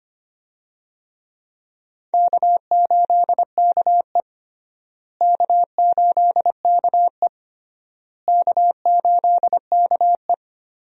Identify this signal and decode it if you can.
Morse code, heard as a keyed tone, about 25 wpm, 700 Hz. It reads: K8KE K8KE K8KE
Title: USA Call Signs - 4 Characters - Repeated 3x - 25wpm